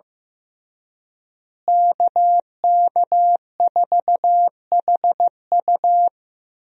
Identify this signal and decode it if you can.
Morse code, heard as a keyed tone, about 15 wpm, 700 Hz. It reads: KK4HU
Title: USA Call Signs - 5 Characters - Repeated 3x - 15wpm